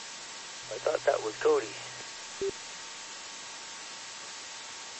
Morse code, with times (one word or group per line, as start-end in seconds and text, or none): none